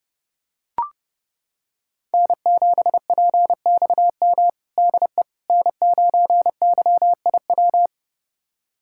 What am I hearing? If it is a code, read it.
Morse code, heard as a keyed tone, about 32 wpm, 700 Hz. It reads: N7PXM DE N9YIW
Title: USA Callsign DE Callsign - 5 Characters - 30wpm